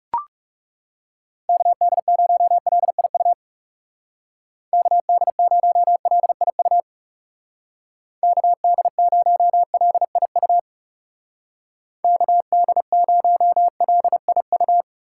Morse code, heard as a keyed tone, about 37 wpm, 700 Hz. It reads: KD0LIU KD0LIU KD0LIU KD0LIU
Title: Speed Racing - USA 2x3 Call Signs 30wpm